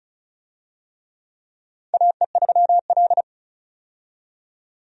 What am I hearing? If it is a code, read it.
Morse code, heard as a keyed tone, about 36 wpm, 700 Hz. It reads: AE3L
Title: USA Call Signs - 4 Characters - 35wpm